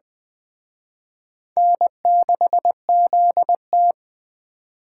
Morse code, heard as a keyed tone, about 20 wpm, 700 Hz. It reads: N6ZT
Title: Speed Racing - USA 1x2 Call Signs 17wpm